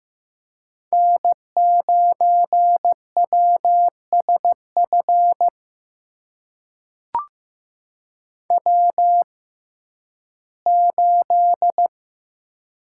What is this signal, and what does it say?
Morse code, heard as a keyed tone, about 15 wpm, spaced as for 11 wpm, 700 Hz. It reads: N9WSF W8
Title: USA Call Signs - 6x Character Spacing - 15wpm